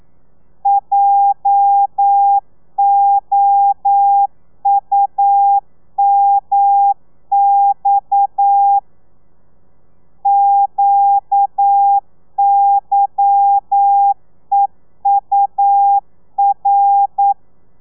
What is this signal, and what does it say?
Morse code, heard as a keyed tone, about 9 wpm, 800 Hz. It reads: JOUMX QYEUR
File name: telegrafia / i2tfj / Lez11 40